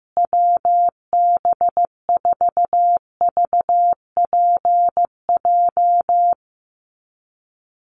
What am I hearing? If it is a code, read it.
Morse code, heard as a keyed tone, about 15 wpm, 700 Hz. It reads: WB4VPJ
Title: USA Call Signs Repeated 2x 15wpm